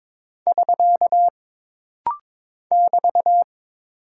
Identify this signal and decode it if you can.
Morse code, heard as a keyed tone, about 22 wpm, 700 Hz. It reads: <SK> <BT>
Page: This rapid-fire timing is geared to push your ICR (Instant Character Recognition) skills.